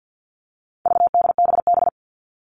Morse code, heard as a keyed tone, about 48 wpm, 700 Hz. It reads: VDBB
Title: All Cut Numbers - 4-Digit 50wpm